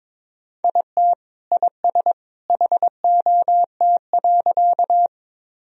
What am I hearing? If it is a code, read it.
Morse code, heard as a keyed tone, about 22 wpm, 700 Hz. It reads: IT IS HOT.
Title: Sentences from Top 1000 Words - Review Part 3 - 22wpm